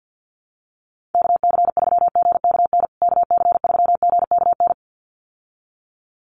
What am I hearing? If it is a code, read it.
Morse code, heard as a keyed tone, about 50 wpm, 700 Hz. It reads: KC3GKN KC3GKN
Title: USA Call Signs Repeated 2x 50wpm